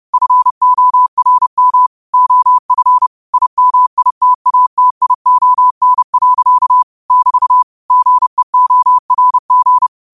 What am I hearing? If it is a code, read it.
Morse code, heard as a keyed tone, about 30 wpm, 1000 Hz. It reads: FORM OF IMITATION. <BT> GEORG
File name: qotdfiles / 30wpm